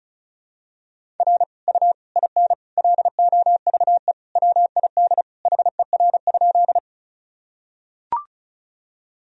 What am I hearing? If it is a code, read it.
Morse code, heard as a keyed tone, about 34 wpm, 700 Hz. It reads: R U IN LOVE WID HER?
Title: Sentences from Top 400 Words and Abbreviations 35wpm